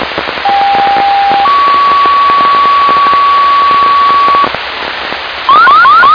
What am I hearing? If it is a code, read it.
Two-tone page, heard as a one-second tone, then a three-second tone, about 800 then 1130 Hz, with noise. It is heard from 0.5 to 4.5 s.